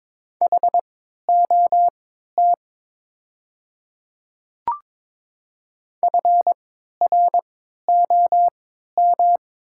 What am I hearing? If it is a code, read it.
Morse code, heard as a keyed tone, about 23 wpm, spaced as for 13 wpm, 700 Hz. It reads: HOT FROM
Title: 3x Character Spacing - Top 100 Words - 22wpm